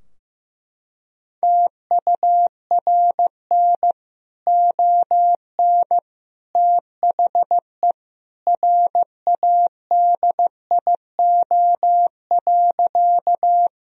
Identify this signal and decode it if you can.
Morse code, heard as a keyed tone, about 15 wpm, 700 Hz. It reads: TURN ON THE RADIO.
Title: Sentences from Top 1000 Words - Review Part 3 - 15wpm